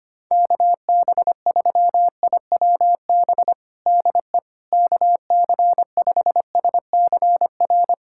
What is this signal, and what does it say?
Morse code, heard as a keyed tone, about 25 wpm, 700 Hz. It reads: KB3IWB DE KC5SCR
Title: USA Callsign DE Callsign - 6 Characters - 25wpm